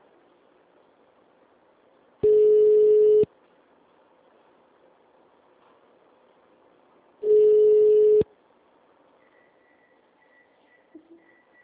Ringback tone (Europe, 425 Hz).